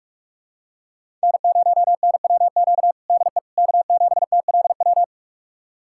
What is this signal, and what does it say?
Morse code, heard as a keyed tone, about 42 wpm, 700 Hz. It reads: N0NWQ DE K7TLW
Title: USA Callsign DE Callsign - 5 Characters - 45wpm